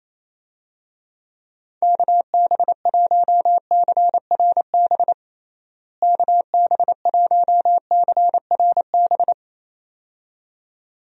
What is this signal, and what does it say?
Morse code, heard as a keyed tone, about 28 wpm, 700 Hz. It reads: KB1CRB KB1CRB
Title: USA Call Signs - 6 Characters - Repeated 2x - 28wpm